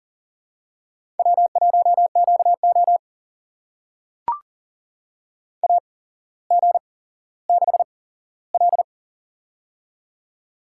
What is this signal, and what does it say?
Morse code, heard as a keyed tone, about 40 wpm, spaced as for 17 wpm, 700 Hz. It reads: W1QO AG6L
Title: USA Call Signs - 4 Characters - 8x Character Spacing - 40wpm